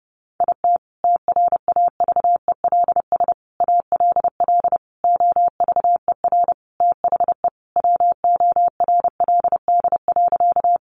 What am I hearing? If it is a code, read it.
Morse code, heard as a keyed tone, about 30 wpm, 700 Hz. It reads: IT TRAVELS ALL OVER THE WORLD.
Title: Additional Easy Phrases 30wpm